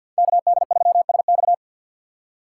Morse code, heard as a keyed tone, about 53 wpm, 700 Hz. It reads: KD3SX
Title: USA Call Signs - 5 Characters - 3x Character Spacing - 50wpm